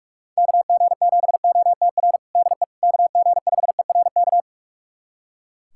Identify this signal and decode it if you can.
Morse code, heard as a keyed tone, about 44 wpm, 690 Hz. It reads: KG7OTR DE KG5ERK
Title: USA Callsign DE Callsign - 6 Characters - 45wpm